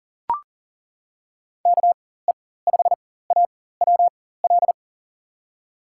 Morse code, heard as a keyed tone, about 40 wpm, spaced as for 19 wpm, 700 Hz. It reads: KE5AWL
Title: USA Call Signs - 6 Characters - 4x Character Spacing - 40wpm